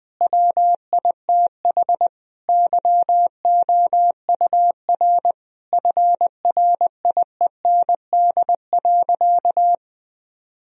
Morse code, heard as a keyed tone, about 20 wpm, 700 Hz. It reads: WITH YOUR FRIEND.
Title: Sentences from Top 1000 Words - Review Part 4 - 20wpm